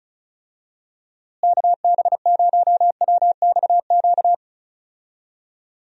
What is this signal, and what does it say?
Morse code, heard as a keyed tone, about 34 wpm, 700 Hz. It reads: KB0WXQ
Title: USA Call Signs - 6 Characters - Repeated 2x - 35wpm